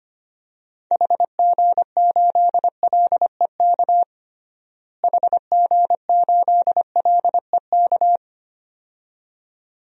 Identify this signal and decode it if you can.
Morse code, heard as a keyed tone, about 25 wpm, 700 Hz. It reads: HG8LEK HG8LEK
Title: DXCC-100 Call Signs - 6 Characters - Repeated 2x - 25wpm